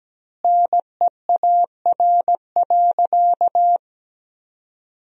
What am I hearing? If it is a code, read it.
Morse code, heard as a keyed tone, about 17 wpm, 700 Hz. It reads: NEAR.
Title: 4x Word-Spacing - Sentences from Top 200 Words - 17wpm